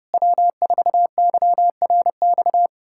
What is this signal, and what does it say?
Morse code, heard as a keyed tone, about 33 wpm, 700 Hz. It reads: W4YRX